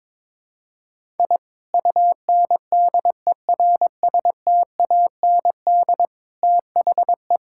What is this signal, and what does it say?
Morse code, heard as a keyed tone, about 22 wpm, 700 Hz. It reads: I UNDERSTAND THE
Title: Sentences from Top 1000 Words - Review Part 6 - 22wpm